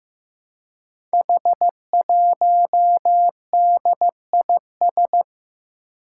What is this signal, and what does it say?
Morse code, heard as a keyed tone, about 15 wpm, 700 Hz. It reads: H1DIS
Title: Five Challenging Letter-Number Combinations 15wpm